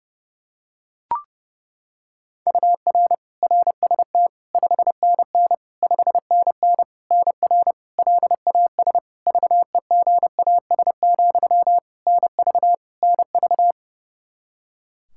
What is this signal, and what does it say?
Morse code, heard as a keyed tone, about 30 wpm, 700 Hz. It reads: UR RST 5NN 5NN NR LAS VEGAS, NV NV
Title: K1USN SST QSO Practice 30wpm